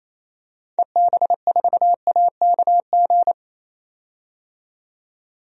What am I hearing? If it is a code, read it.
Morse code, heard as a keyed tone, about 28 wpm, 700 Hz. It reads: EB4AKG
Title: DXCC-100 Call Signs - 6 Characters - 28wpm